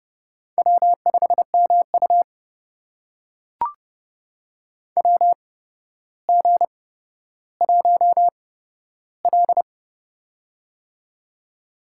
Morse code, heard as a keyed tone, about 30 wpm, spaced as for 13 wpm, 700 Hz. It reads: W5MU WG1L
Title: USA Call Signs - 4 Characters - 8x Character Spacing - 30wpm